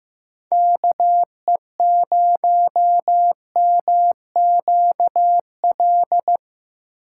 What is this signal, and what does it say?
Morse code, heard as a keyed tone, about 15 wpm, 700 Hz. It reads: KE0MQL
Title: USA Call Signs - 6 Characters - 8x Character Spacing - 15wpm